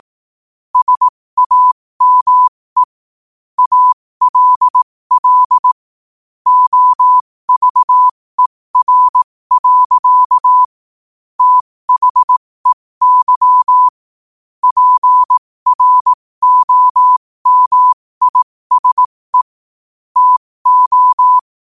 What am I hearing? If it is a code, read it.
Morse code, heard as a keyed tone, about 18 wpm, spaced as for 15 wpm, 1000 Hz. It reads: SAME ALL OVER. THEY PROMISE TO
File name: qotdfiles / 15wpm